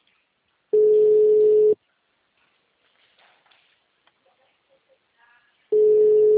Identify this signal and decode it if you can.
Ringback tone (Europe, 425 Hz).